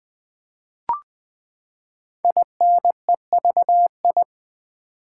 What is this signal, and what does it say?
Morse code, heard as a keyed tone, about 20 wpm, 700 Hz. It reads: INEVI
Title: Build A Word - 5 Letters At A Time - 10-Letter Words - 20wpm